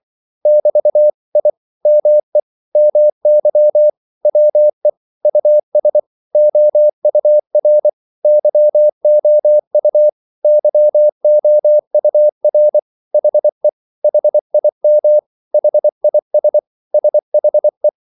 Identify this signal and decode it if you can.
Morse code, heard as a keyed tone, about 24 wpm, 600 Hz. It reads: <BT> I ME MY WE US OUR YOU YOUR HE HIM HIS SHE
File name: CW / Practice / 500MostCommonEnglishWords24